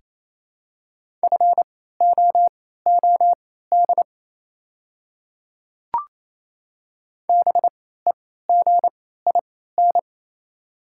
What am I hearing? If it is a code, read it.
Morse code, heard as a keyed tone, about 28 wpm, spaced as for 16 wpm, 700 Hz. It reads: FOOD BEGIN
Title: 3x Character Spacing - Top 300 Words - 28wpm